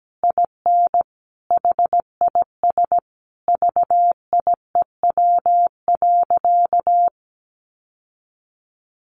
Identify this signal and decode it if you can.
Morse code, heard as a keyed tone, about 17 wpm, 700 Hz. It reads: IN HIS VIEW.
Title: Sentences from Top 1400 Words 17wpm